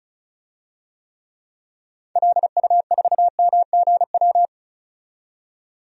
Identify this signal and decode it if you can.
Morse code, heard as a keyed tone, about 34 wpm, 700 Hz. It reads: LU4MGW
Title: DXCC-100 Call Signs - 6 Characters - 35wpm